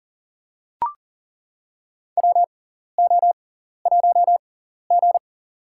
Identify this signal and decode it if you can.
Morse code, heard as a keyed tone, about 40 wpm, spaced as for 14 wpm, 700 Hz. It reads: WO1G
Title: USA Call Signs - 4 Characters - 6x Character Spacing - 40wpm